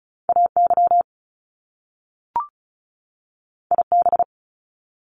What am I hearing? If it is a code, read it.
Morse code, heard as a keyed tone, about 36 wpm, 700 Hz. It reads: AY IB
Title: Top Two Letter Combinations in Words 35wpm